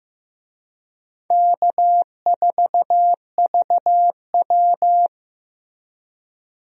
Morse code, heard as a keyed tone, about 15 wpm, 700 Hz. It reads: K4VW
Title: USA Call Signs - 4 Characters - Repeated 2x - 15wpm